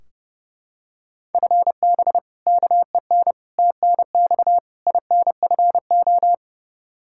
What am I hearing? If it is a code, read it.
Morse code, heard as a keyed tone, about 30 wpm, 700 Hz. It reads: FB KEN TNX INFO
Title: Practice QSO - Exchange 3 - More Info - 30wpm